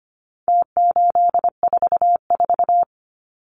Morse code, heard as a keyed tone, about 25 wpm, 700 Hz. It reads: T844
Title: Numbers - Contest Style - 4-Digit 25wpm